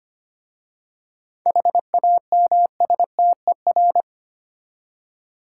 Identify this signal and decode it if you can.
Morse code, heard as a keyed tone, about 25 wpm, 700 Hz. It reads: HAMSTER
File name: practice / 25wpm / Easy Words with Hints - 25wpm